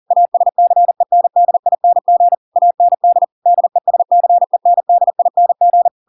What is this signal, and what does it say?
Morse code, heard as a keyed tone, about 40 wpm, 700 Hz. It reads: ASCENDING AND DESCENDING